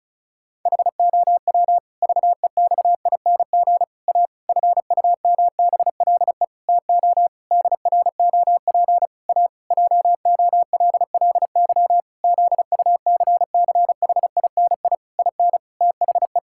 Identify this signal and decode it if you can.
Morse code, heard as a keyed tone, about 34 wpm, 700 Hz. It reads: HOW VEXING A FUMBLE TO DROP A JOLLY ZUCCHINI IN THE
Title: Pangram Challenge v2 - 35wpm